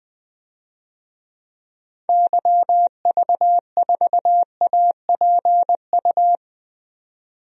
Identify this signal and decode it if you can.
Morse code, heard as a keyed tone, about 20 wpm, 700 Hz. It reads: YV4APU
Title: DXCC-100 Call Signs - 6 Characters - 20wpm